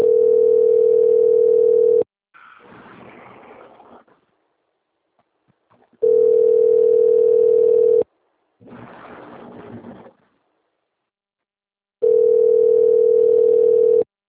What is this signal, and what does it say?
Ringback tone (North America), 440 + 480 Hz.
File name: out- - - - - .8861